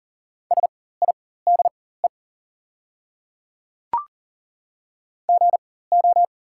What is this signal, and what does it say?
Morse code, heard as a keyed tone, about 37 wpm, spaced as for 18 wpm, 700 Hz. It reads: SIDE GO
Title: 4x Character Spacing - Top 100 Words - 40wpm